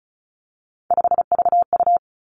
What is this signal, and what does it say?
Morse code, heard as a keyed tone, about 36 wpm, 700 Hz. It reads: 5VU
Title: Three Challenging Letter-Number Combinations 35wpm